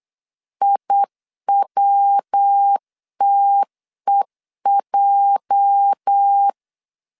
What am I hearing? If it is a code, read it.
Morse code, heard as a keyed tone, about 8 wpm, 800 Hz. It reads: IWTEJ
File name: LEZIONI / 10 SIGNS / 10B INT 8wpm 3534